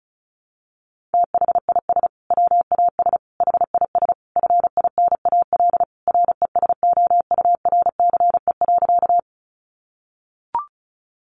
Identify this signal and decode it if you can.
Morse code, heard as a keyed tone, about 34 wpm, 700 Hz. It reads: THIS WAS HIS FINAL RESOURCE.
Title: Sentences from Top 1300 Words 35wpm